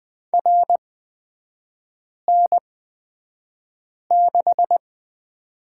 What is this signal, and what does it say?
Morse code, heard as a keyed tone, about 20 wpm, spaced as for 5 wpm, 700 Hz. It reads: RN6
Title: 243 Lesson 39e - Letter z - Three Characters 20wpm